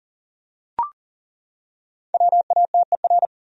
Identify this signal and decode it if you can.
Morse code, heard as a keyed tone, about 40 wpm, 700 Hz. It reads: WATER
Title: Top 100 Words 40wpm